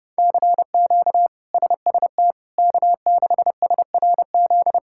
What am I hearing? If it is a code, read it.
Morse code, heard as a keyed tone, about 30 wpm, 700 Hz. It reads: CQ SST K6SRZ
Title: K1USN SST QSO Practice - 30wpm